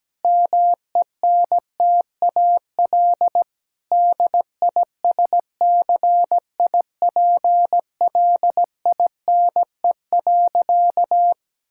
Morse code, heard as a keyed tone, about 17 wpm, 700 Hz. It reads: MENTAL DISCIPLINE.
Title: Sentences from Top 1000 Words - Review Part 6 - 17wpm